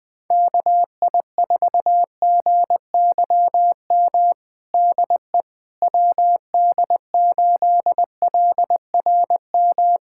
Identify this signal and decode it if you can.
Morse code, heard as a keyed tone, about 20 wpm, 700 Hz. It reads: KI4GYM DE WD8LRM
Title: USA Callsign DE Callsign - 6 Characters - 20wpm